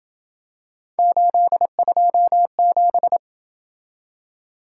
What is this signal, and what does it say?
Morse code, heard as a keyed tone, about 27 wpm, 700 Hz. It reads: 827